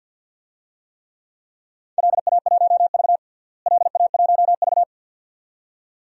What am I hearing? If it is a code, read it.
Morse code, heard as a keyed tone, about 51 wpm, 700 Hz. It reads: LA1V LA1V
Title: DXCC-100 Call Signs - 4 Characters - Repeated 2x - 50wpm